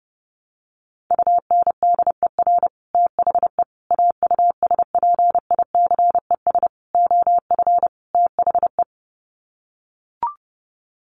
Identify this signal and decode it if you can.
Morse code, heard as a keyed tone, about 30 wpm, 700 Hz. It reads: UNDER THE AUSPICES OF THE
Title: Sets of 5 Words Encore2 30wpm